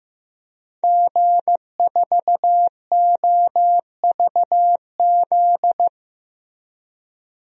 Morse code, heard as a keyed tone, about 15 wpm, 700 Hz. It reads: G4OVZ
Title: DXCC-100 Call Signs - 5 Characters - 15wpm